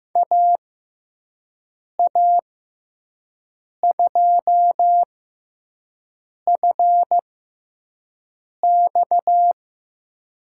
Morse code, heard as a keyed tone, about 15 wpm, spaced as for 5 wpm, 700 Hz. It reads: AA2FX
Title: USA Call Signs - 5 Characters - 6x Character Spacing - 15wpm